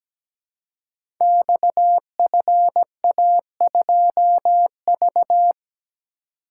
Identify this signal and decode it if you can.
Morse code, heard as a keyed tone, about 17 wpm, 700 Hz. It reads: XFA2V
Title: Code Groups - 5 Letter:Numbers - 17wpm